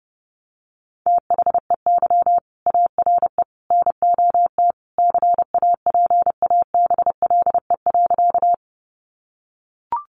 Morse code, heard as a keyed tone, about 30 wpm, 700 Hz. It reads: THEY ARE NOT CAPABLE.
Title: Sentences from Top 1200 Words 30wpm